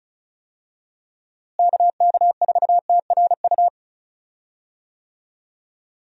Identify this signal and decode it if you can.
Morse code, heard as a keyed tone, about 36 wpm, 700 Hz. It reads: KK4TRU
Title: USA Call Signs - 6 Characters - Repeated 3x - 35wpm